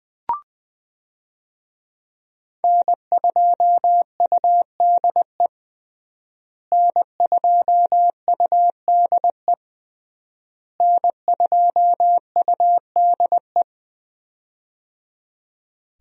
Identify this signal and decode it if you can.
Morse code, heard as a keyed tone, about 20 wpm, 700 Hz. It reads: N2UDE N2UDE N2UDE
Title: USA Call Signs - 5 Characters - Repeated 3x - 20wpm